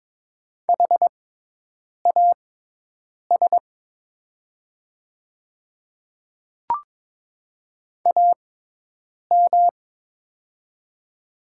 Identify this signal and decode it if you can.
Morse code, heard as a keyed tone, about 22 wpm, spaced as for 8 wpm, 700 Hz. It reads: HAS AM
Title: 6x Character Spacing - Top 500 Words - 22wpm